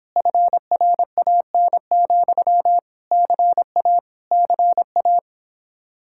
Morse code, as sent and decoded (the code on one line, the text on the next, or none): ..-. .-. .- -. --..-- / -.-. .- / -.-. .-
FRAN, CA CA